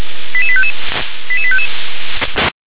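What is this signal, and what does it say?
Five-tone selective call (ZVEI), heard as quick runs of five tones, tones of about 70 ms. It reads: 88855 88855
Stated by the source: ZVEI five-tone selective call received by SDR.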